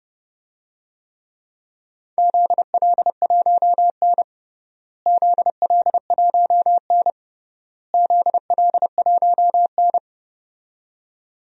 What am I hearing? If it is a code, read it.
Morse code, heard as a keyed tone, about 32 wpm, 700 Hz. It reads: ZL1N ZL1N ZL1N
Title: DXCC-100 Call Signs - 4 Characters - Repeated 3x - 30wpm